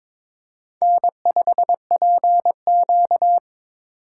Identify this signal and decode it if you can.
Morse code, heard as a keyed tone, about 23 wpm, 700 Hz. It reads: N5PQ
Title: USA Call Signs - 3x Character Spacing - 22wpm